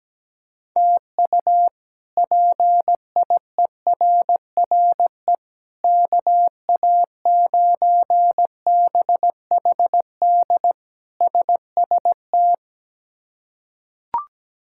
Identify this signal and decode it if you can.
Morse code, heard as a keyed tone, about 17 wpm, 700 Hz. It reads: TU PIERRE KA9BHD SST
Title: K1USN SST QSO Practice - 17wpm